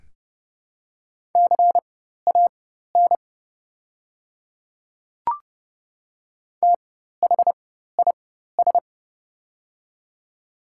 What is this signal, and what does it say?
Morse code, heard as a keyed tone, about 29 wpm, spaced as for 14 wpm, 700 Hz. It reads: CAN THIS
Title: 4x Character Spacing - Top 200 Words - 30wpm